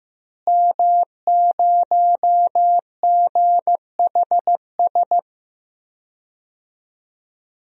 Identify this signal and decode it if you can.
Morse code, heard as a keyed tone, about 15 wpm, 700 Hz. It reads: M0GHS
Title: DXCC-100 Call Signs - 5 Characters - Repeated 2x - 15wpm